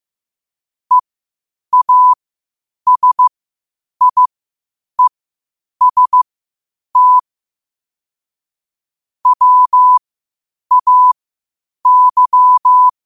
Morse code, heard as a keyed tone, about 15 wpm, spaced as for 8 wpm, 1000 Hz. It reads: EASIEST WAY
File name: 8wpm